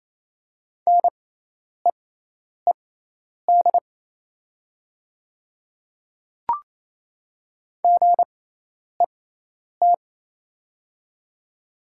Morse code, heard as a keyed tone, about 28 wpm, spaced as for 10 wpm, 700 Hz. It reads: NEED GET
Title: 6x Character Spacing - Top 200 Words - 28wpm